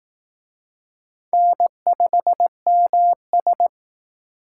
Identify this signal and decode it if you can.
Morse code, heard as a keyed tone, about 19 wpm, 700 Hz. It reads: N5MS
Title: Speed Racing - USA 1x2 Call Signs 15wpm